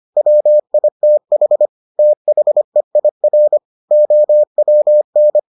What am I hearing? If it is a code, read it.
Morse code, heard as a keyed tone, about 25 wpm, 600 Hz. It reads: WITH THEIR OWN